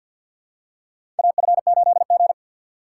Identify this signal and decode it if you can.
Morse code, heard as a keyed tone, about 54 wpm, 700 Hz. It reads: AF8G
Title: USA Call Signs - 4 Characters - 6x Character Spacing - 50wpm